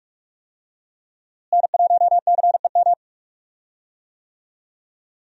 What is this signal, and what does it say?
Morse code, heard as a keyed tone, about 41 wpm, 700 Hz. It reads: N1CEM
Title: USA Call Signs - 5 Characters - Repeated 2x - 45wpm